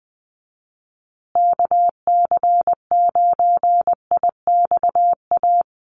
Morse code, heard as a keyed tone, about 20 wpm, 700 Hz. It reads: KC9IXA
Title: USA Call Signs Repeated 3x 20wpm